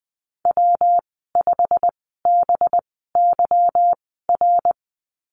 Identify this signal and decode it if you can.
Morse code, heard as a keyed tone, about 20 wpm, spaced as for 15 wpm, 700 Hz. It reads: W5BYR
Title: 200 Lesson 32g - Letter k - Call Signs 20wpm